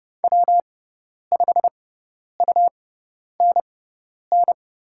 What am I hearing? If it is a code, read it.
Morse code, heard as a keyed tone, about 30 wpm, spaced as for 10 wpm, 700 Hz. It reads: W5UNN
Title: USA Call Signs - 5 Characters - 6x Character Spacing - 30wpm